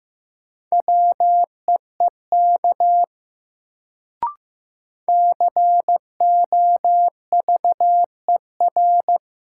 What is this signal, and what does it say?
Morse code, heard as a keyed tone, about 15 wpm, 700 Hz. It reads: WEEK COVER
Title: Sentences from Top 500 Words - New Word Drill - 15wpm